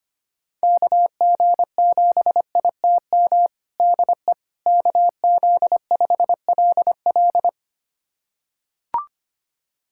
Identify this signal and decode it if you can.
Morse code, heard as a keyed tone, about 25 wpm, 700 Hz. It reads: KG7ITM DE KZ5LL
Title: USA Callsign DE Callsign 25wpm